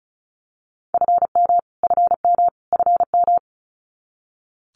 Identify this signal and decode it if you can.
Morse code, heard as a keyed tone, about 34 wpm, 700 Hz. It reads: FM FM FM
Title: Abbreviations Repeated 3x - 35wpm